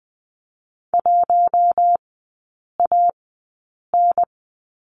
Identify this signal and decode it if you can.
Morse code, heard as a keyed tone, about 20 wpm, spaced as for 8 wpm, 700 Hz. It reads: 1AN
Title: 060 Lesson 11e - Letter h - Three Characters 20wpm